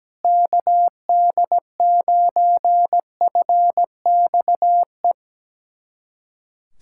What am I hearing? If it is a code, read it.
Morse code, heard as a keyed tone, about 17 wpm, 700 Hz. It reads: KD9FXE
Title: SOTA QSO Practice 17wpm